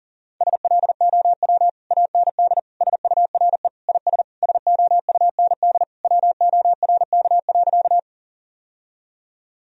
Morse code, heard as a keyed tone, about 37 wpm, 700 Hz. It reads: SLOW AND SURE IS SOUND WORK.
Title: Sentences from Top 1000 Words - Review Part 2 - 40wpm